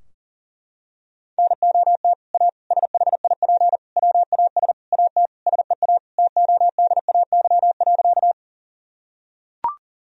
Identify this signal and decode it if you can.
Morse code, heard as a keyed tone, about 40 wpm, 700 Hz. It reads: NOT A SHIP WAS AT SEA TODAY.
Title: Sentences from Top 1000 Words - Review Part 2 - 40wpm